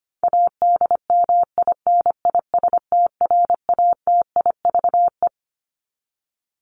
Morse code, heard as a keyed tone, about 25 wpm, 700 Hz. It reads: ADMINISTRATIVE
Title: Top 1501-1750 Words 25wpm